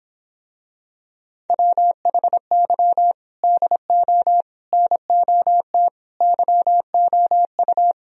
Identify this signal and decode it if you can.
Morse code, heard as a keyed tone, about 26 wpm, 700 Hz. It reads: WHY DO NOT YOU